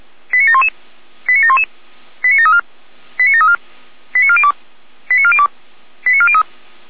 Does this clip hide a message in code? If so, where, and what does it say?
Yes: five-tone selective call (ZVEI) from 0.3 to 6.4 s, heard as quick runs of five tones, tones of about 80 ms. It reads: 89711 89711 89734 89734 89402 89402 89403